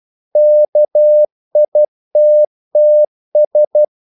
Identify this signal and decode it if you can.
Morse code, heard as a keyed tone, about 12 wpm, 600 Hz. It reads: KITTS
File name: CW / Practice / NationalCapitals12